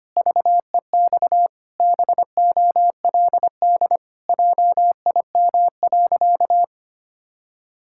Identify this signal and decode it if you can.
Morse code, heard as a keyed tone, about 25 wpm, 700 Hz. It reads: VEX BOLD JIM.
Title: Pangram Challenge v2 - 25wpm